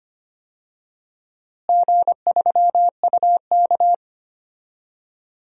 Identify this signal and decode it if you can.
Morse code, heard as a keyed tone, about 25 wpm, 700 Hz. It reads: G3UK
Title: DXCC-100 Call Signs - 4 Characters - Repeated 3x - 25wpm